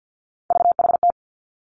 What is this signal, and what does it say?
Morse code, heard as a keyed tone, about 50 wpm, 700 Hz. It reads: VHT
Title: Three Challenging Letter-Number Combinations 50wpm